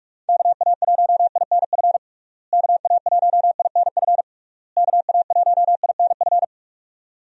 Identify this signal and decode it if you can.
Morse code, heard as a keyed tone, about 46 wpm, 700 Hz. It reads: KA1INF KA1INF KA1INF
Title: USA Call Signs - 6 Characters - Repeated 3x - 45wpm